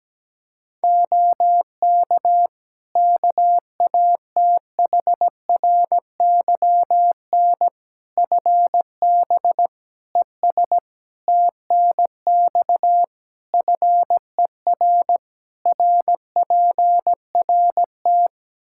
Morse code, heard as a keyed tone, about 17 wpm, 700 Hz. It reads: OK KATHRYN FB ES TNX FER RPRT
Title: K1USN SST QSO Practice 17wpm